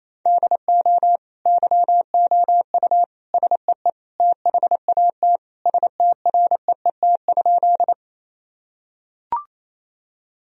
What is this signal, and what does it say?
Morse code, heard as a keyed tone, about 28 wpm, 700 Hz. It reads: DO YOU SEE THAT STREET?
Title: Sentences from Top 1000 Words - Review Part 3 - 28wpm